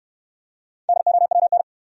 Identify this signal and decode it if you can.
Morse code, heard as a keyed tone, about 69 wpm, 700 Hz. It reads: DOWN